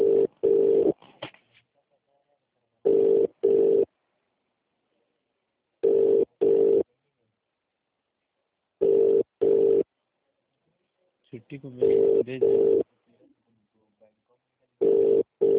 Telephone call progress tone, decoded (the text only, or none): ringback tone